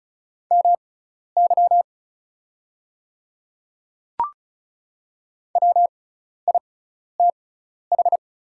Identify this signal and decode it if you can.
Morse code, heard as a keyed tone, about 36 wpm, spaced as for 12 wpm, 690 Hz. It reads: MY WITH
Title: 6x Character Spacing - Top 100 Words - 35wpm